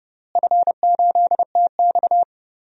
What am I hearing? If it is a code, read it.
Morse code, heard as a keyed tone, about 30 wpm, 700 Hz. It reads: F8TX
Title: DXCC-100 Call Signs - 4 Characters - Repeated 2x - 30wpm